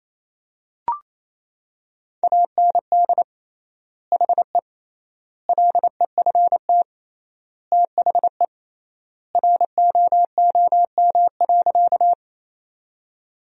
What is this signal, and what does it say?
Morse code, heard as a keyed tone, about 28 wpm, 700 Hz. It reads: AND HE LEFT THE ROOM.
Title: 3x Word-Spacing - Sentences from Top 300 Words - 28wpm